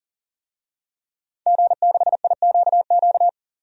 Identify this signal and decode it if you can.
Morse code, heard as a keyed tone, about 40 wpm, 700 Hz. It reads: G6IQQ
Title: DXCC-100 Call Signs - 5 Characters - 40wpm